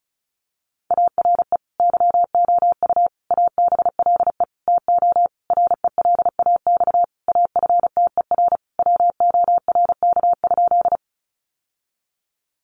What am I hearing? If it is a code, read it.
Morse code, heard as a keyed tone, about 34 wpm, 700 Hz. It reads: ARE YOU ABLE TO RELAX AFTER WORK?
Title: Sentences from Top 1400 Words 35wpm